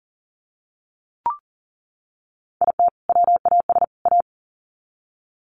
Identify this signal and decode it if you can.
Morse code, heard as a keyed tone, about 40 wpm, 700 Hz. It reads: IT WAS A
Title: Sets of 3 Words 40wpm